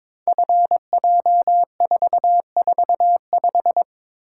Morse code, heard as a keyed tone, about 22 wpm, 700 Hz. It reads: FJ445
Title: Code Groups - 5 Letter:Numbers - 22wpm